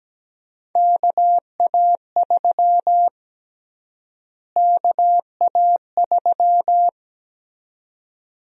Morse code, heard as a keyed tone, about 17 wpm, 700 Hz. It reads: KA3 KA3
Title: USA Call Sign Prefixes - Repeated 2x - 17wpm